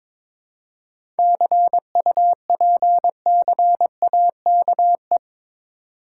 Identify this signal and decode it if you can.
Morse code, heard as a keyed tone, about 23 wpm, 700 Hz. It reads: CUPCAKE
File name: Difficult Words with Hints - 22wpm